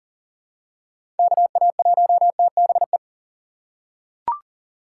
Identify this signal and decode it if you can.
Morse code, heard as a keyed tone, about 37 wpm, 700 Hz. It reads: KA1TBE
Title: USA Call Signs - 6 Characters - 3x Character Spacing - 40wpm